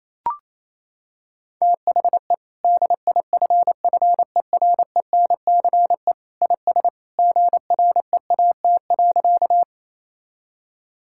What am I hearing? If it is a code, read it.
Morse code, heard as a keyed tone, about 28 wpm, 700 Hz. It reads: THE DIFFERENCE IS GREAT.
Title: Sentences from Top 1000 Words - Review Part 5 - 28wpm